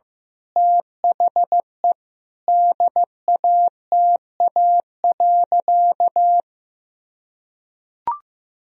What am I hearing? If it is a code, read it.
Morse code, heard as a keyed tone, about 16 wpm, 700 Hz. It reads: THE DATA.
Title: Sentences from Top 1000 Words - Review Part 3 - 15wpm